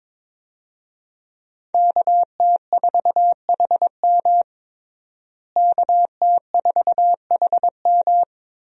Morse code, heard as a keyed tone, about 22 wpm, 700 Hz. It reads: KT4HM KT4HM
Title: USA Call Signs - 5 Characters - Repeated 2x - 22wpm